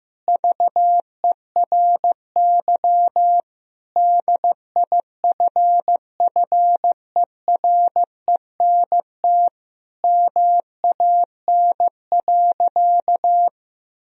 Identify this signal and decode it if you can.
Morse code, heard as a keyed tone, about 15 wpm, 700 Hz. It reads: VERY DIFFERENT MAN.
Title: Sentences from Top 1000 Words - Review Part 3 - 15wpm